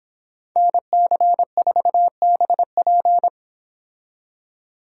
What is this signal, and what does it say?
Morse code, heard as a keyed tone, about 25 wpm, 700 Hz. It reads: NC4BP